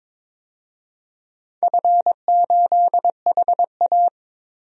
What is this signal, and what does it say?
Morse code, heard as a keyed tone, about 23 wpm, 700 Hz. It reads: F8HA
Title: DXCC-100 Call Signs - 4 Characters - 22wpm